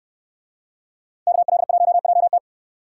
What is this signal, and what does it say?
Morse code, heard as a keyed tone, about 69 wpm, 690 Hz. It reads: KL1JT